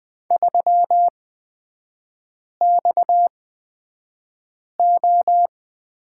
Morse code, heard as a keyed tone, about 20 wpm, spaced as for 5 wpm, 700 Hz. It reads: 3XO
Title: 229 Lesson 37e - Letter x - Three Characters 20wpm